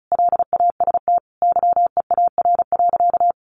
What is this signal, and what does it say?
Morse code, heard as a keyed tone, about 34 wpm, 700 Hz. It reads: LAST YEAR.
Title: Sentences from Top 1200 Words 35wpm